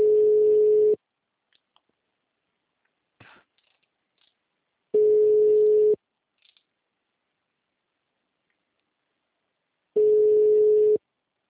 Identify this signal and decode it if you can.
Ringback tone (Europe, 425 Hz).
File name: out- -2027- - - .46690